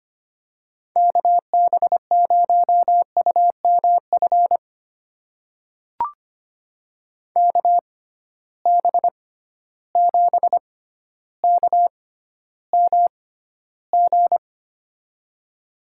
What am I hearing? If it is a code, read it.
Morse code, heard as a keyed tone, about 25 wpm, spaced as for 13 wpm, 700 Hz. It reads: KB0UMF KB7KMG
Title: USA Call Signs - 6 Characters - 6x Character Spacing - 25wpm